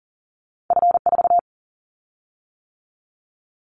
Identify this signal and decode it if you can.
Morse code, heard as a keyed tone, about 40 wpm, 700 Hz. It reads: F4
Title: Callsign Prefixes for Top 50 DXCC - 40wpm